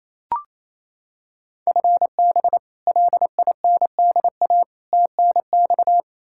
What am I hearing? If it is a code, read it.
Morse code, heard as a keyed tone, about 28 wpm, 700 Hz. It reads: FB LINDA TNX
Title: Practice QSO - Exchange 3 - More Info - 28wpm